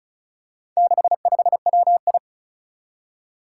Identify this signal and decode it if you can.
Morse code, heard as a keyed tone, about 36 wpm, 700 Hz. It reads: 65WI
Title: Four Challenging Letter-Number Combinations 35wpm